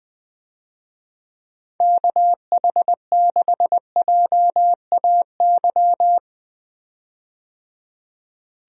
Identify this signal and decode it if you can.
Morse code, heard as a keyed tone, about 20 wpm, 700 Hz. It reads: KH6JAY
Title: DXCC-100 Call Signs - 6 Characters - 20wpm